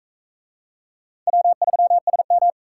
Morse code, heard as a keyed tone, about 42 wpm, 700 Hz. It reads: W3SM